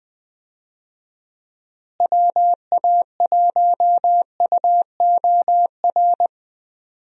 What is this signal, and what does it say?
Morse code, heard as a keyed tone, about 20 wpm, 700 Hz. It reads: WA1UOR
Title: USA Call Signs - 6 Characters - Repeated 3x - 20wpm